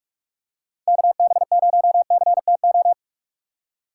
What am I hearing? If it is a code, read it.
Morse code, heard as a keyed tone, about 47 wpm, 700 Hz. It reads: KB0CTO
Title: USA Call Signs - 6 Characters - 4x Character Spacing - 45wpm